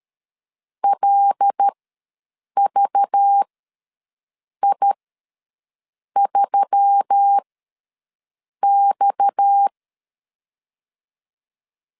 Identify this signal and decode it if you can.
Morse code, heard as a keyed tone, about 13 wpm, spaced as for 6 wpm, 800 Hz. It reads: LVI3X
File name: LEZIONI / 7 GF9Z / 7 GF9Z ws 12wpm 3321